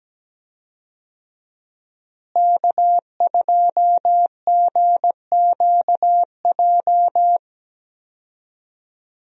Morse code, heard as a keyed tone, about 17 wpm, 700 Hz. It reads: K2GQJ
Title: USA Call Signs - 5 Characters - Repeated 2x - 17wpm